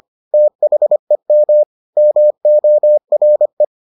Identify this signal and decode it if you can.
Morse code, heard as a keyed tone, about 25 wpm, 600 Hz. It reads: THEM MORE